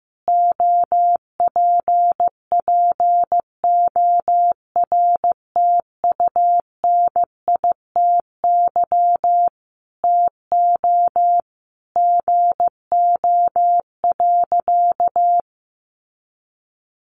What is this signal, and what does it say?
Morse code, heard as a keyed tone, about 15 wpm, 700 Hz. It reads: OPPORTUNITY TO GO.
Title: Sentences from Top 1000 Words 15wpm